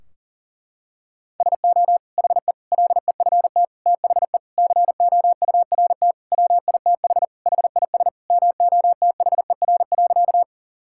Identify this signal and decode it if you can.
Morse code, heard as a keyed tone, about 37 wpm, 700 Hz. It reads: SO HE LEFT THE COURT WITH HIS MOTHER.
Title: Sentences from Top 1600 Words 40wpm